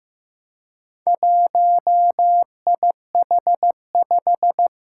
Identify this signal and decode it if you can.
Morse code, heard as a keyed tone, about 15 wpm, 700 Hz. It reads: 1IH5
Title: Four Challenging Letter-Number Combinations 15wpm